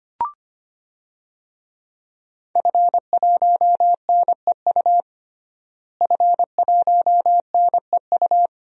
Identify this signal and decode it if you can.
Morse code, heard as a keyed tone, about 25 wpm, 700 Hz. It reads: F1NEU F1NEU
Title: DXCC-100 Call Signs - 5 Characters - Repeated 2x - 25wpm